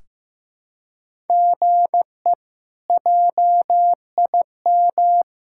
Morse code, heard as a keyed tone, about 15 wpm, 700 Hz. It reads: GE JIM
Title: K1USN SST QSO Practice - 15wpm